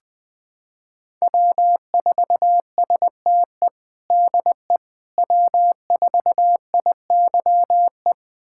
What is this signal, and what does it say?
Morse code, heard as a keyed tone, about 20 wpm, 700 Hz. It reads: W4STE DE W4IYE
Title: USA Callsign DE Callsign - 5 Characters - 20wpm